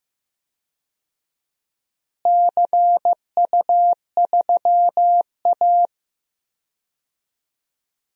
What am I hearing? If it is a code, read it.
Morse code, heard as a keyed tone, about 15 wpm, 700 Hz. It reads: CU3A
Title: DXCC-100 Call Signs - 4 Characters - Repeated 2x - 15wpm